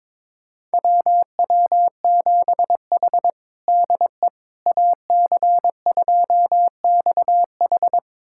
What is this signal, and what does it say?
Morse code, heard as a keyed tone, about 22 wpm, 700 Hz. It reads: WW7H DE AC2XH
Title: USA Callsign DE Callsign 22wpm